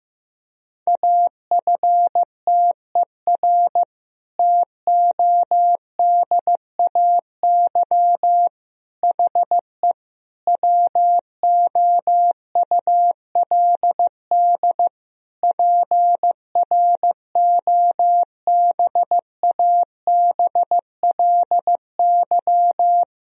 Morse code, heard as a keyed tone, about 15 wpm, 700 Hz. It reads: AFTER TODAY HE WOULD PROBABLY
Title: Sentences from Top 1000 Words - Review Part 4 - 15wpm